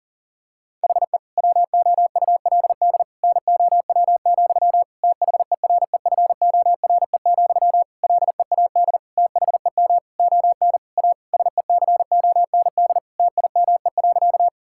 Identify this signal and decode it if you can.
Morse code, heard as a keyed tone, about 40 wpm, 700 Hz. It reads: HE WOULD NOW, THEREFORE, LEAD THEM ON A SECOND TIME.
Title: Sentences from Top 1000 Words - Review Part 5 - 40wpm